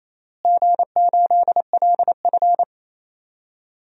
Morse code, heard as a keyed tone, about 28 wpm, 700 Hz. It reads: G8LF
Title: DXCC-100 Call Signs - 4 Characters - Repeated 3x - 28wpm